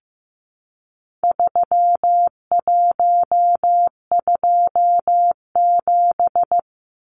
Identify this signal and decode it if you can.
Morse code, heard as a keyed tone, about 15 wpm, 700 Hz. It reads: 3127
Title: Numbers - 4-Digit 15wpm